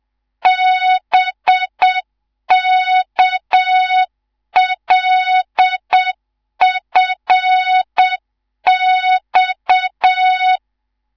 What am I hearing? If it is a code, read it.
Morse code, heard as a keyed tone, about 7 wpm, 3700 Hz, with noise. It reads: BKLFX
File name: Morze-HA2MN / hang / 19 GYAKORLAT